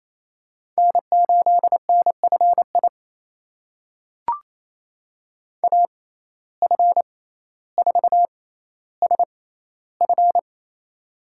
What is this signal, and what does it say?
Morse code, heard as a keyed tone, about 28 wpm, spaced as for 14 wpm, 700 Hz. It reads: N8NFI AF4SF
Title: USA Call Signs - 5 Characters - 6x Character Spacing - 28wpm